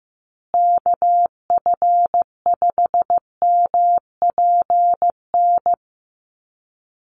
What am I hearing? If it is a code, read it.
Morse code, heard as a keyed tone, about 15 wpm, 700 Hz. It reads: KF5MPN
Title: USA Call Signs Repeated 2x 15wpm